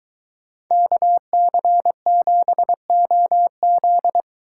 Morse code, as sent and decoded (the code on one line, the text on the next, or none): -.- -.-. --... --- --..
KC7OZ